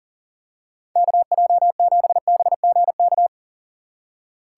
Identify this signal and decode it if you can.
Morse code, heard as a keyed tone, about 40 wpm, 700 Hz. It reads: KJ7BGK
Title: USA Call Signs - 6 Characters - 3x Character Spacing - 40wpm